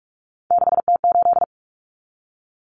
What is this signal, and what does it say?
Morse code, heard as a keyed tone, about 44 wpm, 700 Hz. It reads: 6T8
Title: Common Cut Numbers - 3-Digit 45wpm